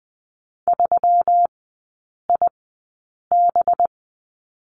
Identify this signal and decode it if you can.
Morse code, heard as a keyed tone, about 20 wpm, spaced as for 8 wpm, 700 Hz. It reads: 3IB
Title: 184 Lesson 30e - Letter b - Three Characters 20wpm